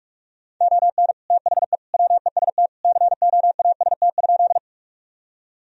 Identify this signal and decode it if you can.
Morse code, heard as a keyed tone, about 41 wpm, 700 Hz. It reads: ON THE WEST COAST?
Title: Sentences from Top 1000 Words - Review Part 4 - 45wpm